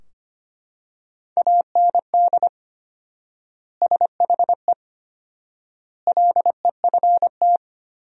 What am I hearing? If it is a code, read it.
Morse code, heard as a keyed tone, about 25 wpm, 700 Hz. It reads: AND SHE LEFT
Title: 4x Word-Spacing - Sentences from Top 300 Words - 25wpm